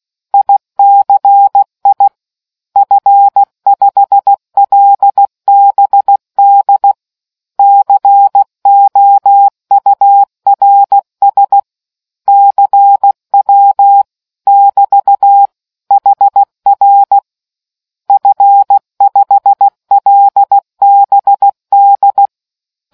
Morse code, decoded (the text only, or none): ICI F5LBD COURS CW <BT> HR F5LBD